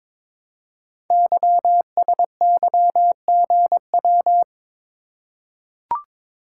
Morse code, heard as a keyed tone, about 22 wpm, 700 Hz. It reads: YSYGW
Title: Code Groups - 5 Letters - 22wpm